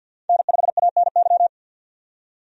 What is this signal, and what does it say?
Morse code, heard as a keyed tone, about 45 wpm, 700 Hz. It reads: N5ANY
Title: USA Call Signs - 5 Characters - 50wpm